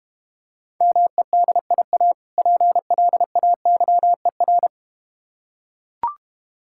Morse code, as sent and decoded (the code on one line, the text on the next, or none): -- . -.. .. .- / .--. .-.. .- -.-- . .-.
MEDIA PLAYER